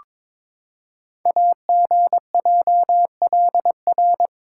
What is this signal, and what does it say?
Morse code, heard as a keyed tone, about 22 wpm, 700 Hz. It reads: AGJLR
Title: Code Groups - 5 Letters - 22wpm